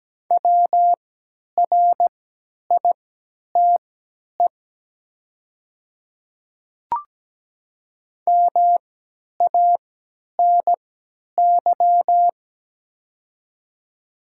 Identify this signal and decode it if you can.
Morse code, heard as a keyed tone, about 17 wpm, spaced as for 10 wpm, 690 Hz. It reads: WRITE MANY
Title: 3x Character Spacing - Top 100 Words - 17wpm